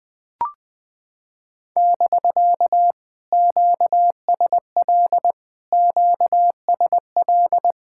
Morse code, heard as a keyed tone, about 19 wpm, 700 Hz. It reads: <BK> QSL QSL
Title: POTA QSO Practice - 20wpm